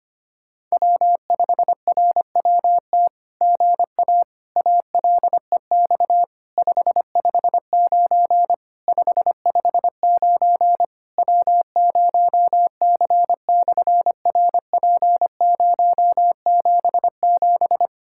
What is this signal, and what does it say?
Morse code, heard as a keyed tone, about 25 wpm, 700 Hz. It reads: W5RWT GA ALEX 559 559 W0C/RP077
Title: SOTA QSO Practice 25wpm